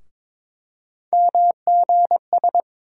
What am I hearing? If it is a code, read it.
Morse code, heard as a keyed tone, about 22 wpm, 700 Hz. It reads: MGS
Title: USA Call Sign Postfixes - 3 Letters 22wpm